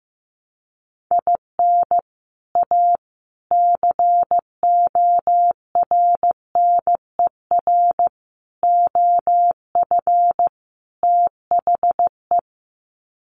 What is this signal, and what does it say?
Morse code, heard as a keyed tone, about 15 wpm, 700 Hz. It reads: IN A CORNER OF THE
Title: Sets of 5 Words Encore2 15wpm